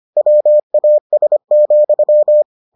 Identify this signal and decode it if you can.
Morse code, heard as a keyed tone, about 25 wpm, 600 Hz. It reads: WAS,